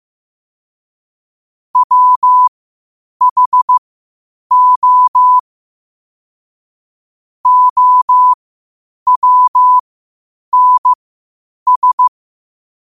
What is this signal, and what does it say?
Morse code, heard as a keyed tone, about 15 wpm, spaced as for 8 wpm, 1000 Hz. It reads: WHO OWNS
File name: qotdfiles / 8wpm